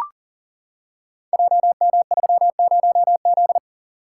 Morse code, heard as a keyed tone, about 40 wpm, 690 Hz. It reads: JM30Z
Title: Code Groups - 5 Letter:Numbers - 40wpm